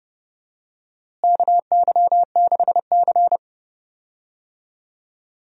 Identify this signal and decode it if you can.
Morse code, heard as a keyed tone, about 31 wpm, 700 Hz. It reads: KY6C
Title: USA Call Signs - 4 Characters - Repeated 2x - 30wpm